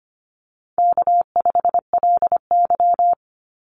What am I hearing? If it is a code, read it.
Morse code, heard as a keyed tone, about 25 wpm, 700 Hz. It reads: K5LY
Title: Call Signs Repeated 2x - 25wpm